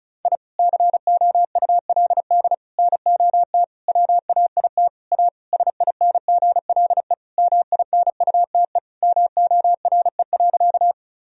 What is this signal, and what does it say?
Morse code, heard as a keyed tone, about 36 wpm, 700 Hz. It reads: I COULD NOT WAIT A SINGLE MINUTE MORE.
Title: Sentences from Top 1000 Words - Review Part 4 - 35wpm